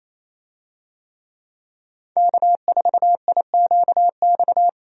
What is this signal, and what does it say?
Morse code, heard as a keyed tone, about 28 wpm, 700 Hz. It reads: K4IQX
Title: USA Call Signs - 5 Characters - 28wpm